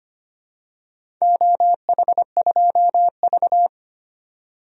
Morse code, heard as a keyed tone, about 25 wpm, 700 Hz. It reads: OH2V
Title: DXCC-100 Call Signs - 4 Characters - 25wpm